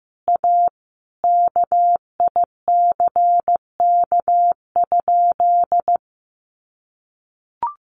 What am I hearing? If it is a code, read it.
Morse code, heard as a keyed tone, about 15 wpm, 700 Hz. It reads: A KICK?
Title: Sentences from Top 1300 Words 15wpm